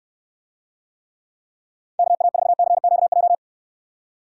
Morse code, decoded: KA4CQJ

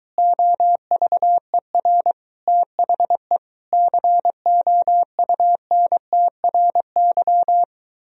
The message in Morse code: OVER THE COUNTRY

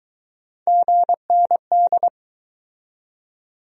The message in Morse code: GND

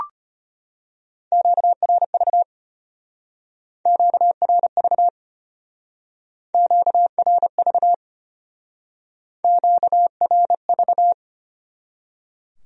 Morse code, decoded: QRV QRV QRV QRV

